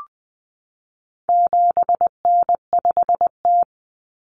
Morse code, heard as a keyed tone, about 20 wpm, 700 Hz. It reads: 7N5T
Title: Common Cut Numbers - 4-Digit 20wpm